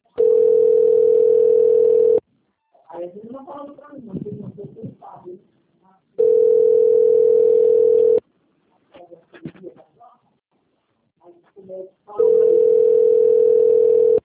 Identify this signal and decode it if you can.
Ringback tone (North America), 440 + 480 Hz, with speech.
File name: rg-1600-unknown- - - .73245